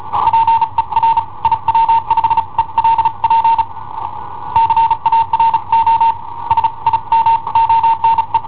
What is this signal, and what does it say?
Morse code, heard as a keyed tone, about 35 wpm, spaced as for 30 wpm, 910 Hz. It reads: PER IW5ELP CARO SIMONE
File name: i7ohp / cassetta i7ohp B